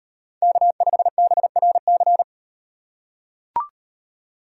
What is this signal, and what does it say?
Morse code, heard as a keyed tone, about 38 wpm, 700 Hz. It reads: K5BRC